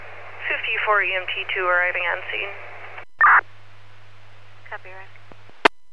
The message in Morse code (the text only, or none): none